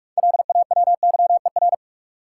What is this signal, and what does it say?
Morse code, heard as a keyed tone, about 45 wpm, 700 Hz. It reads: LAWYER
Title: Six Letter Words - 45wpm